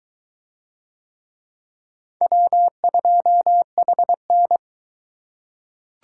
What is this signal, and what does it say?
Morse code, heard as a keyed tone, about 21 wpm, 700 Hz. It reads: W2HN